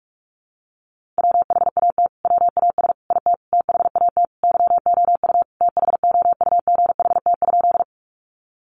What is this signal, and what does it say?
Morse code, heard as a keyed tone, about 44 wpm, 700 Hz. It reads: WHAT WAS IT THAT YOU THOUGHT?
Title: Sentences from Top 300 Words 45wpm